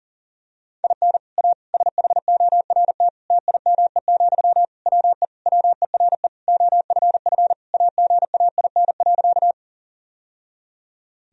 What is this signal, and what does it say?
Morse code, heard as a keyed tone, about 40 wpm, 700 Hz. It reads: IN A SHORT TIME, WE WERE OFF AGAIN.
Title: Sentences from Top 1000 Words - Review Part 2 - 40wpm